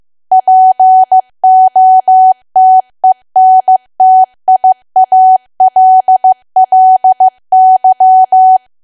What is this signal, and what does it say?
Morse code, heard as a keyed tone, about 15 wpm, 750 Hz. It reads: POTENTIALLY
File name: cwa / W1AW-Level2 / 15WPM